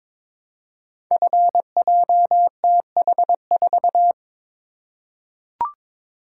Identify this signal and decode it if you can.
Morse code, heard as a keyed tone, about 22 wpm, 700 Hz. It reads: FJTH4
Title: Code Groups - 5 Letter:Numbers - 22wpm